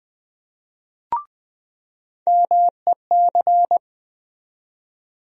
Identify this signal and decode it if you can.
Morse code, heard as a keyed tone, about 20 wpm, 700 Hz. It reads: MEC
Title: USA Call Sign Postfixes - 3 Letters 20wpm